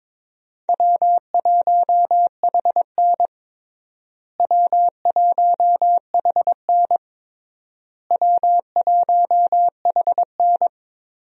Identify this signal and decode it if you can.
Morse code, heard as a keyed tone, about 22 wpm, 700 Hz. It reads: W1HN W1HN W1HN
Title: USA Call Signs - 4 Characters - Repeated 3x - 22wpm